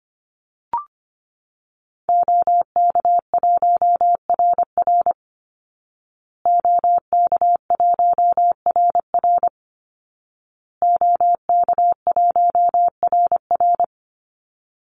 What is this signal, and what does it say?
Morse code, heard as a keyed tone, about 25 wpm, 700 Hz. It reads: OK1RR OK1RR OK1RR
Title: Call Signs Repeated 2x - 25wpm